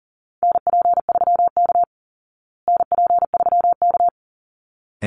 Morse code, heard as a keyed tone, about 37 wpm, 700 Hz. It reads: NP3K NP3K
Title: Call Signs Repeated 3x - 40wpm